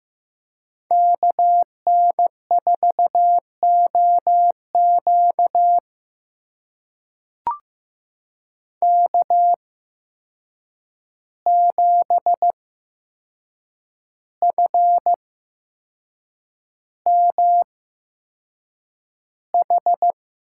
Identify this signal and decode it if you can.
Morse code, heard as a keyed tone, about 15 wpm, spaced as for 6 wpm, 700 Hz. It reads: KN4OQ K7FMH
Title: USA Call Signs - 5 Characters - 8x Character Spacing - 15wpm